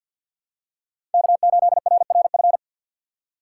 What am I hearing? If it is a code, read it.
Morse code, heard as a keyed tone, about 52 wpm, 700 Hz. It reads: K8RRF
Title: USA Call Signs - 5 Characters - 3x Character Spacing - 50wpm